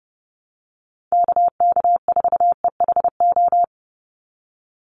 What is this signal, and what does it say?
Morse code, heard as a keyed tone, about 30 wpm, 700 Hz. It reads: KK4EHO
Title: USA Call Signs 30wpm